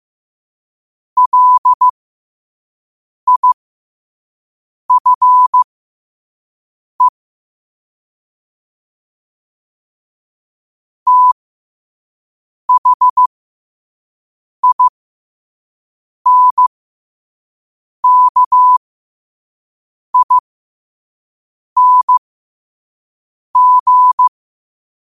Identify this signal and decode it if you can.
Morse code, heard as a keyed tone, about 15 wpm, spaced as for 5 wpm, 1000 Hz. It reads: LIFE THINKING
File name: qotdfiles / 5wpm